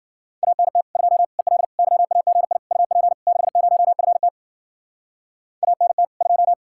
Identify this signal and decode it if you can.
Morse code, heard as a keyed tone, about 49 wpm, 700 Hz. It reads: ANT 2 EL YAGI UP 60FT ANT 2
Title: Speed-Racing Common QSO Phrases - 40wpm